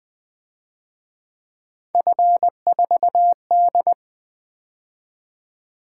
Morse code, heard as a keyed tone, about 23 wpm, 700 Hz. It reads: F4D
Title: Speed-Racing - Three Letter-Number Combinations - 20wpm